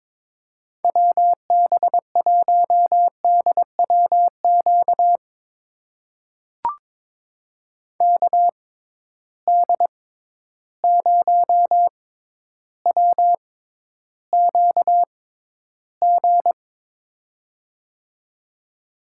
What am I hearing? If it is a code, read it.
Morse code, heard as a keyed tone, about 23 wpm, spaced as for 11 wpm, 700 Hz. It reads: WB1DWQ KD0WQG
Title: USA Call Signs - 6 Characters - 6x Character Spacing - 22wpm